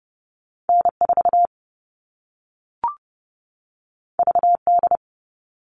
Morse code, heard as a keyed tone, about 30 wpm, 700 Hz. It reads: N4 VD
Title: All Cut Numbers - 2-Digit 30wpm